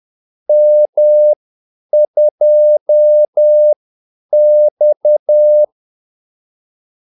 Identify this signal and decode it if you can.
Morse code, heard as a keyed tone, about 10 wpm, spaced as for 8 wpm, 600 Hz. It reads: M2X
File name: Callsigns-10-8wpm